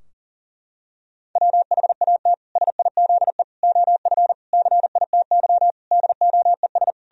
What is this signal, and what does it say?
Morse code, heard as a keyed tone, about 40 wpm, 700 Hz. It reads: WHAT SIZE OF CITY DOES
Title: Sentences from Top 1000 Words - Review Part 2 - 40wpm